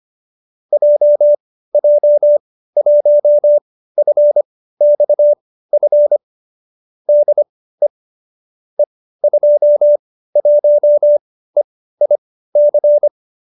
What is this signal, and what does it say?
Morse code, heard as a keyed tone, about 25 wpm, spaced as for 15 wpm, 590 Hz. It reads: JJ1FXF DE E21EIC